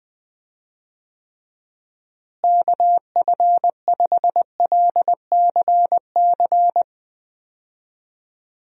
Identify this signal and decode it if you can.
Morse code, heard as a keyed tone, about 20 wpm, 700 Hz. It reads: KF5LCC
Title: USA Call Signs - 6 Characters - Repeated 2x - 20wpm